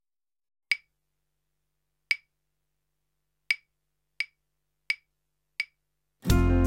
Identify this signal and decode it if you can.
There is no code.